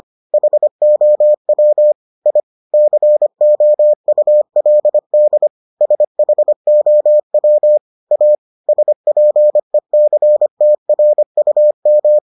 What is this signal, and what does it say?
Morse code, heard as a keyed tone, about 25 wpm, 600 Hz. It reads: HOW I COULD SHOW A SPECTRUM